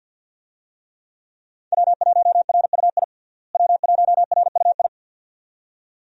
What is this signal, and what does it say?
Morse code, heard as a keyed tone, about 53 wpm, 700 Hz. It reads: W1RUI W1RUI
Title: USA Call Signs - 5 Characters - Repeated 2x - 50wpm